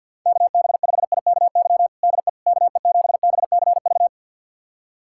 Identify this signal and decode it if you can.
Morse code, heard as a keyed tone, about 52 wpm, 700 Hz. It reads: KB5IKY DE KE7BCV
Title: USA Callsign DE Callsign - 6 Characters - 50wpm